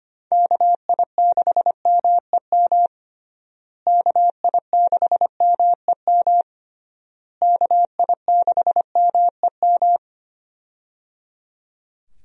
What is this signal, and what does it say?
Morse code, heard as a keyed tone, about 25 wpm, 700 Hz. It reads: KI6MEM KI6MEM KI6MEM
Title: USA Call Signs - 6 Characters - Repeated 3x - 25wpm